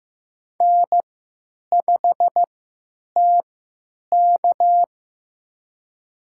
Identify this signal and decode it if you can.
Morse code, heard as a keyed tone, about 15 wpm, spaced as for 9 wpm, 700 Hz. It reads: N5TK
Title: USA Call Signs - 4 Characters - 3x Character Spacing - 15wpm